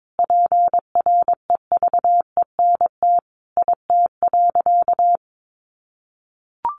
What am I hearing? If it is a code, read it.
Morse code, heard as a keyed tone, about 22 wpm, 700 Hz. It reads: PREVENT IT.
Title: Sentences from Top 1100 Words 22wpm